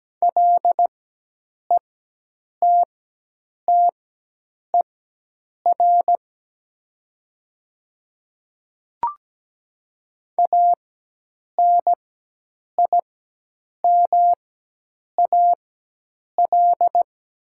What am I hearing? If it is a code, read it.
Morse code, heard as a keyed tone, about 17 wpm, spaced as for 8 wpm, 700 Hz. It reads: LETTER ANIMAL
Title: 4x Character Spacing - Top 300 Words - 17wpm